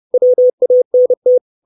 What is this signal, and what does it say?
Morse code, heard as a keyed tone, about 30 wpm, 500 Hz. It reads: WANT
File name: 632 Words 30WPM